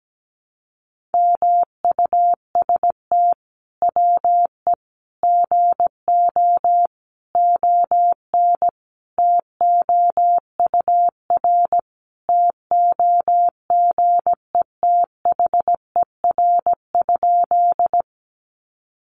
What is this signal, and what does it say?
Morse code, heard as a keyed tone, about 17 wpm, 700 Hz. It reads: MUST WE GO ON TOUR TOGETHER?
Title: Sentences from Top 1400 Words 17wpm